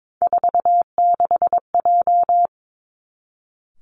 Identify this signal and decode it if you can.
Morse code, heard as a keyed tone, about 22 wpm, 700 Hz. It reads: 46J
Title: Three Challenging Letter-Number Combinations 22wpm